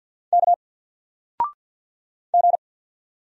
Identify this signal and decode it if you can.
Morse code, heard as a keyed tone, about 50 wpm, 700 Hz. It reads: K G
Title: Single Character (Letter:Number:Punctuation:Prosign) 50wpm